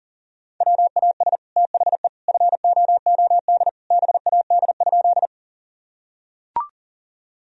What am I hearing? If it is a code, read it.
Morse code, heard as a keyed tone, about 40 wpm, 700 Hz. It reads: WAS THE FOOD BAD?
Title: Sentences from Top 1000 Words - Review Part 1 - 40wpm